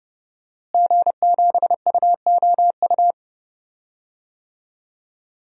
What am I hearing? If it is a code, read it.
Morse code, heard as a keyed tone, about 30 wpm, 700 Hz. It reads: G7UOU
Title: DXCC-100 Call Signs - 5 Characters - Repeated 3x - 30wpm